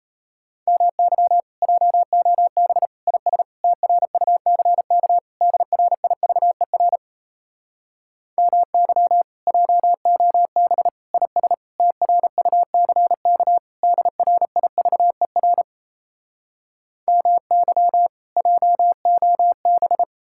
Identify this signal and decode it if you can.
Morse code, heard as a keyed tone, about 37 wpm, 700 Hz. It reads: MY JOB IS TRUCK DRIVER MY JOB IS TRUCK DRIVER MY JOB